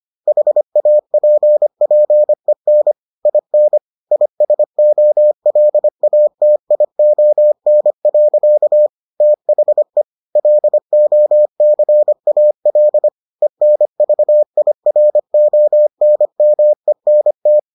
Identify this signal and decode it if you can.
Morse code, heard as a keyed tone, about 25 wpm, 600 Hz. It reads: HAPPEN IN ISOLATION. THE LOCAL ENVIRONMENT